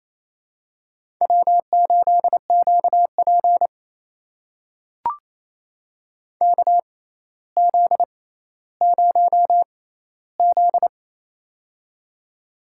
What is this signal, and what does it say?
Morse code, heard as a keyed tone, about 28 wpm, spaced as for 14 wpm, 700 Hz. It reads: W8QP KZ0Z
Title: USA Call Signs - 4 Characters - 6x Character Spacing - 28wpm